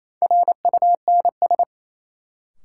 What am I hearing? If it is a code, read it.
Morse code, heard as a keyed tone, about 28 wpm, 700 Hz. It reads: RUNS
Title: Sentences from Top 600 Words - New Word Drill - 28wpm